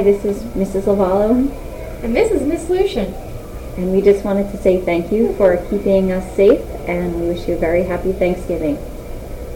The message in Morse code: none